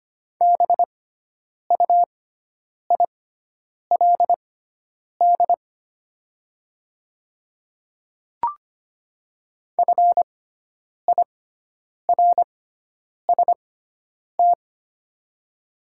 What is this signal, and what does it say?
Morse code, heard as a keyed tone, about 25 wpm, spaced as for 9 wpm, 690 Hz. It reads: BUILD FIRST
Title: 6x Character Spacing - Top 200 Words - 25wpm